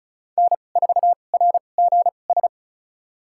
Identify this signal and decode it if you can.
Morse code, heard as a keyed tone, about 36 wpm, spaced as for 25 wpm, 700 Hz. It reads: N4RGS
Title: USA Call Signs - 5 Characters - 2x Character Spacing - 35wpm